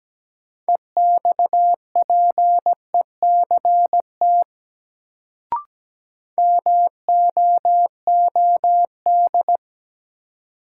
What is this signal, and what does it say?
Morse code, heard as a keyed tone, about 17 wpm, 700 Hz. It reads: EXPECT MOOD
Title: Sentences from Top 1100 Words - New Word Drill Alt - 17wpm